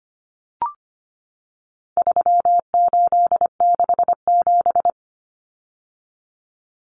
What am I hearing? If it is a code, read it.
Morse code, heard as a keyed tone, about 25 wpm, 700 Hz. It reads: 3867
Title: Numbers - 4-Digit 25wpm